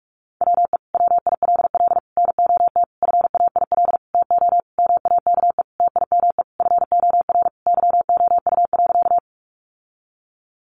Morse code, heard as a keyed tone, about 44 wpm, 700 Hz. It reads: WE WILL NOT FAIL TO MAKE TIME FOR YOU.
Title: Sentences from Top 1100 Words 45wpm